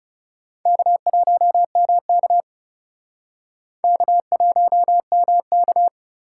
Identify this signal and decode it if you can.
Morse code, heard as a keyed tone, about 37 wpm, 700 Hz. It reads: K1MK K1MK
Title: Speed Racing - USA 1x2 Call Signs 30wpm